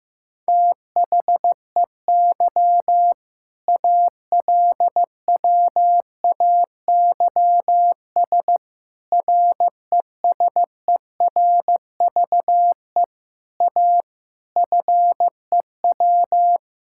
Morse code, heard as a keyed tone, about 15 wpm, 700 Hz. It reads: THEY ALWAYS RESERVE A FEW
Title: Sentences from Top 1000 Words - Review Part 6 - 15wpm